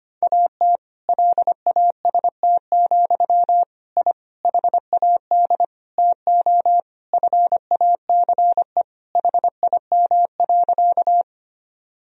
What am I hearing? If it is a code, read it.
Morse code, heard as a keyed tone, about 25 wpm, 700 Hz. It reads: AT LAST, I HAD TO FACE HIM.
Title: Sentences from Top 1000 Words - Review Part 2 - 25wpm